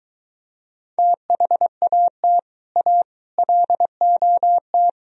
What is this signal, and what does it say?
Morse code, heard as a keyed tone, about 24 wpm, 700 Hz. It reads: THAT A LOT